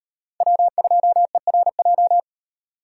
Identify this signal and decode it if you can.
Morse code, heard as a keyed tone, about 37 wpm, 700 Hz. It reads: W2ERJ